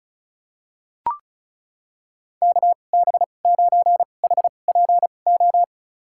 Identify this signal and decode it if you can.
Morse code, heard as a keyed tone, about 36 wpm, spaced as for 25 wpm, 700 Hz. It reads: KB9HPO
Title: USA Call Signs - 6 Characters - 2x Character Spacing - 35wpm